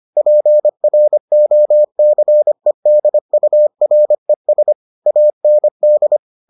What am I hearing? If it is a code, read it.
Morse code, heard as a keyed tone, about 25 wpm, 600 Hz. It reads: PROCEDURES AND